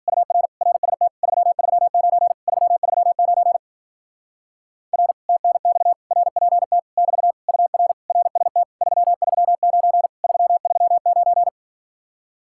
Morse code, heard as a keyed tone, about 52 wpm, 700 Hz. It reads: UR RST 339 339 R TNX RPT <BT> UR RST 339 339